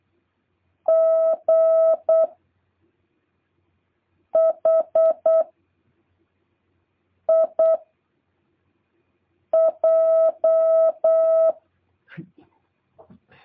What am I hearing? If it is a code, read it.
Morse code, heard as a keyed tone, about 8 wpm, 660 Hz. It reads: G H I J